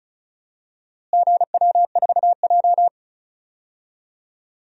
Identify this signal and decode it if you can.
Morse code, heard as a keyed tone, about 36 wpm, 700 Hz. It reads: GW4J
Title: DXCC-100 Call Signs - 4 Characters - Repeated 3x - 35wpm